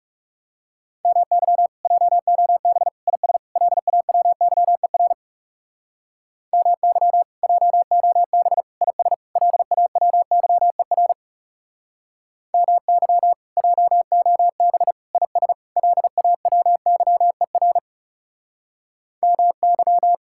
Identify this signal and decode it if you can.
Morse code, heard as a keyed tone, about 40 wpm, 700 Hz. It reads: MY JOB IS LAWYER MY JOB IS LAWYER MY JOB IS LAWYER MY